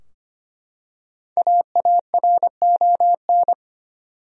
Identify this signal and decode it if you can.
Morse code, heard as a keyed tone, about 25 wpm, 700 Hz. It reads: AARON
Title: USA Common Names - 25wpm